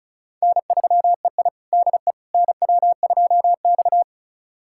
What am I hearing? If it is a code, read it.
Morse code, heard as a keyed tone, about 36 wpm, 690 Hz. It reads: N3EI DE NW2X
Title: USA Callsign DE Callsign - 4 Characters - 35wpm